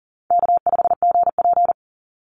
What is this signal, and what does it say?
Morse code, heard as a keyed tone, about 38 wpm, 700 Hz. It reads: K5GP
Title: Easy Call Signs - 40wpm